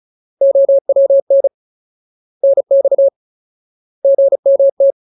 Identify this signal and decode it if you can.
Morse code, heard as a keyed tone, about 34 wpm, 550 Hz. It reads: OWN NX GMT
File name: Ultimate Word List / 35WPM Ultimate